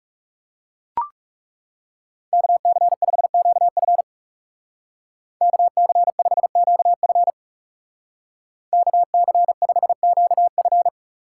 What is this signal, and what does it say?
Morse code, heard as a keyed tone, about 36 wpm, 700 Hz. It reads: KC5QF KC5QF KC5QF
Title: Speed Racing - USA 2x2 Call Signs 30wpm